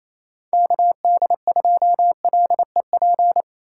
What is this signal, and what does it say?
Morse code, heard as a keyed tone, about 38 wpm, spaced as for 33 wpm, 700 Hz. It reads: KD2LEP
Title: Speed-Racing Common QSO Phrases - 28wpm